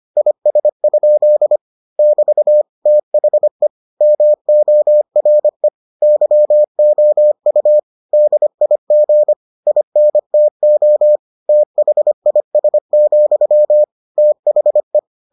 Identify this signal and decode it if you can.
Morse code, heard as a keyed tone, about 25 wpm, 600 Hz. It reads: IS? <BT> THE MORE YOU DIG INTO THIS, THE